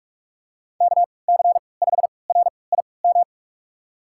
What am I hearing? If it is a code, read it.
Morse code, heard as a keyed tone, about 44 wpm, spaced as for 25 wpm, 700 Hz. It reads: KC5RIM
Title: USA Call Signs - 3x Character Spacing - 45wpm